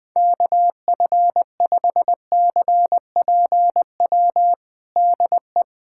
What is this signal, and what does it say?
Morse code, heard as a keyed tone, about 21 wpm, 700 Hz. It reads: KF5CPW DE
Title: USA Callsign DE Callsign - 6 Characters - 20wpm